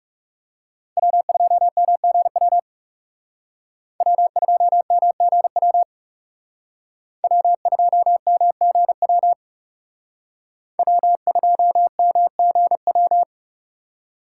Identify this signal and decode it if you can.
Morse code, heard as a keyed tone, about 37 wpm, 700 Hz. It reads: W2MGW W2MGW W2MGW W2MGW
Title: Speed Racing - USA 1x3 Call Signs 30wpm